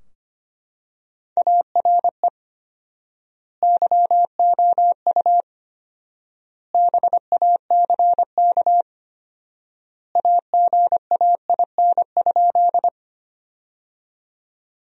Morse code, heard as a keyed tone, about 25 wpm, 700 Hz. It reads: ARE YOU BACK AGAIN?
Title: 4x Word-Spacing - Sentences from Top 200 Words - 25wpm